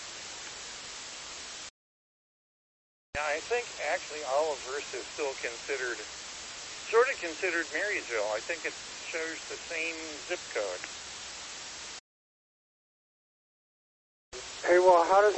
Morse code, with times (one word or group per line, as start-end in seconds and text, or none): none